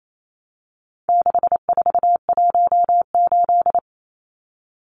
Morse code, heard as a keyed tone, about 28 wpm, 700 Hz. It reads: 6418
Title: Numbers - 4-Digit 28wpm